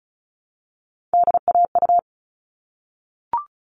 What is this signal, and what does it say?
Morse code, heard as a keyed tone, about 34 wpm, 700 Hz. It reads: DAU
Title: All Cut Numbers - 3-Digit 35wpm